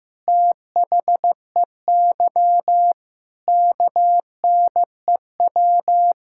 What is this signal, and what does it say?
Morse code, heard as a keyed tone, about 15 wpm, 700 Hz. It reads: THEY KNEW
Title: Sentences from Top 1600 Words 15wpm